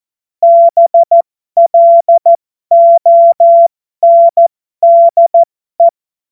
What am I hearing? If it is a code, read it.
Morse code, heard as a keyed tone, about 14 wpm, spaced as for 12 wpm, 680 Hz. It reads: BLONDE
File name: Morsekurs / Witze